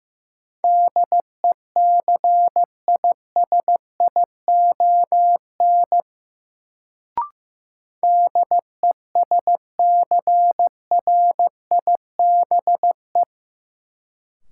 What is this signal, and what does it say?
Morse code, heard as a keyed tone, about 15 wpm, 700 Hz. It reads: DECISION DESCRIBE
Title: Sentences from Top 1000 Words - New Word Drill - 15wpm